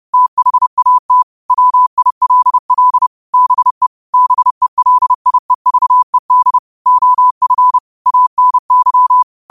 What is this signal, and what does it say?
Morse code, heard as a keyed tone, about 30 wpm, 1000 Hz. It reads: THAT WILL BE BELIEVED OF ANY